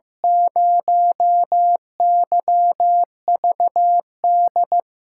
Morse code, heard as a keyed tone, about 15 wpm, 700 Hz. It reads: 0YVD
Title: USA Call Signs - 6 Characters - 8x Character Spacing - 15wpm